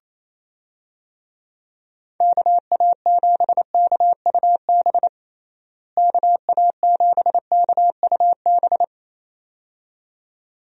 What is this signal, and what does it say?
Morse code, heard as a keyed tone, about 28 wpm, 700 Hz. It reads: KA7KUB KA7KUB
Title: USA Call Signs - 6 Characters - Repeated 2x - 28wpm